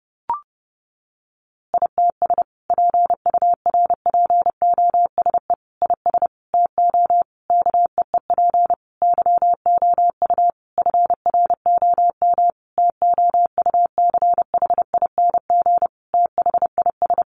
Morse code, heard as a keyed tone, about 30 wpm, 700 Hz. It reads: ITS PURPOSE IS TO KEEP YOU FROM TOUCHING THIS
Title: A Sound Of Thunder 30wpm